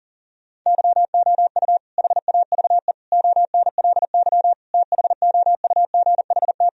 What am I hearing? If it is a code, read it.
Morse code, heard as a keyed tone, about 40 wpm, 700 Hz. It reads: YOU HAVE ONLY THOUGHT
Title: Sentences from Top 1000 Words - Review Part 2 - 40wpm